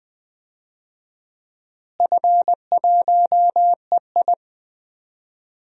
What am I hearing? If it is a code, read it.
Morse code, heard as a keyed tone, about 20 wpm, 700 Hz. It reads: F1EI
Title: DXCC-100 Call Signs - 4 Characters - Repeated 2x - 20wpm